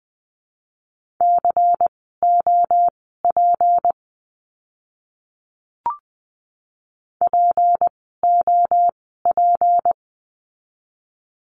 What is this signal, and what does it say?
Morse code, heard as a keyed tone, about 20 wpm, spaced as for 15 wpm, 700 Hz. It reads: COP POP
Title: 153 Lesson 25f - Letter p - Words 20wpm